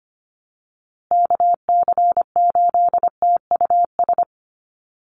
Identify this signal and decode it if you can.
Morse code, heard as a keyed tone, about 25 wpm, 700 Hz. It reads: KC8TUS
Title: USA Call Signs Repeated 2x 25wpm